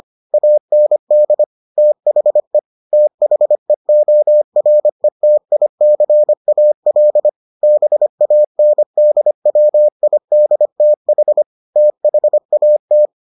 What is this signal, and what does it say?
Morse code, heard as a keyed tone, about 25 wpm, 600 Hz. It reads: AND THE THEORETICAL BANDWIDTH THAT